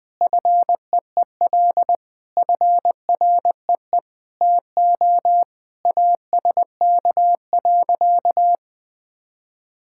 Morse code, heard as a keyed tone, about 20 wpm, 700 Hz. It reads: FEEL FREE TO ASK.
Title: Sentences with 8-Word Phrases - 20wpm